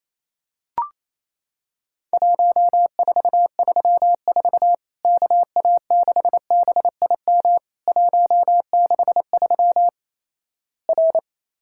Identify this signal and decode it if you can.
Morse code, heard as a keyed tone, about 28 wpm, 700 Hz. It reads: 1434 KA6BIM 163 R
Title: Worked All Europe DX Contest - QTC Exchange - 28wpm